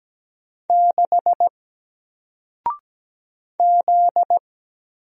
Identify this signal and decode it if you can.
Morse code, heard as a keyed tone, about 17 wpm, 700 Hz. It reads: 6 Z
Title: Single Letter-Number Alt 17wpm